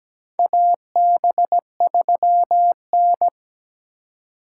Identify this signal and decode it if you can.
Morse code, heard as a keyed tone, about 17 wpm, 700 Hz. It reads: AB3N
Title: USA Call Signs - 4 Characters - 2x Character Spacing - 17wpm